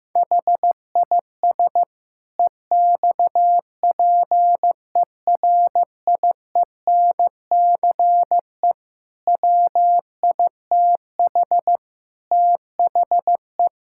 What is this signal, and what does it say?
Morse code, heard as a keyed tone, about 15 wpm, 700 Hz. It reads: HIS EXPERIENCE WITH THE
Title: Sentences from Top 1000 Words - Review Part 6 - 15wpm